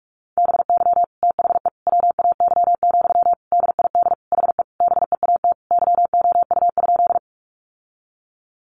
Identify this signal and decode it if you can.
Morse code, heard as a keyed tone, about 44 wpm, 700 Hz. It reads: BY THE WAY, DID HE BEAT YOU?
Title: Sentences from Top 1300 Words 45wpm